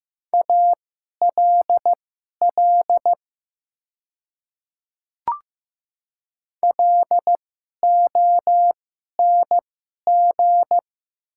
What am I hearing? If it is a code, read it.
Morse code, heard as a keyed tone, about 15 wpm, spaced as for 11 wpm, 700 Hz. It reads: ALL LONG
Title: 2x Character Spacing - Top 100 Words - 15wpm